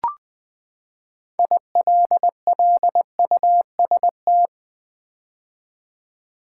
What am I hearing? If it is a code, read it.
Morse code, heard as a keyed tone, about 20 wpm, 700 Hz. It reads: ILLUST
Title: Build A Word - 6 Letters At A Time - 12-Letter Words - 20wpm